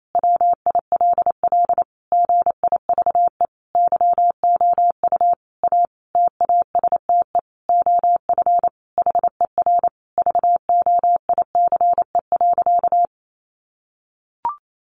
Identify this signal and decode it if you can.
Morse code, heard as a keyed tone, about 28 wpm, 700 Hz. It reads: WILL GIVE YOU A TASTE OF HER VOICE.
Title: Sentences from Top 1100 Words 28wpm